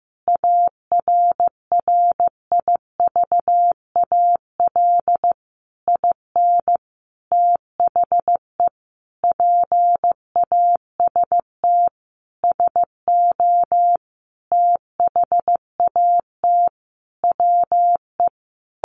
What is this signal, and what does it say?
Morse code, heard as a keyed tone, about 15 wpm, 700 Hz. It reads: ARRIVAL IN THE PAST SO THAT WE
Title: A Sound Of Thunder 15wpm